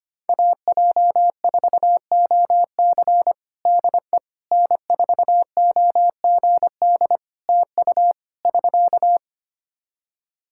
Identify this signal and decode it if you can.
Morse code, heard as a keyed tone, about 25 wpm, 700 Hz. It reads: AJ4OC DE N4OGD TU <SK>
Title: Practice QSO - Exchange 4 - Ending - 25wpm